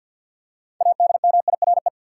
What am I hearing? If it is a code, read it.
Morse code, heard as a keyed tone, about 47 wpm, 700 Hz. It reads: ADMIRE
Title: Six Letter Words - 50wpm